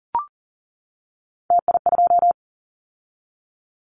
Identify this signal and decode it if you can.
Morse code, heard as a keyed tone, about 40 wpm, 700 Hz. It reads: TI2
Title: Callsign Prefixes for Top 100 DXCC - 40wpm